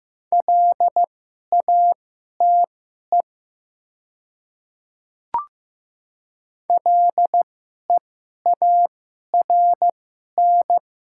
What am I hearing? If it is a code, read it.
Morse code, heard as a keyed tone, about 15 wpm, spaced as for 11 wpm, 700 Hz. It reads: LATE LEARN
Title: 2x Character Spacing - Top 500 Words - 15wpm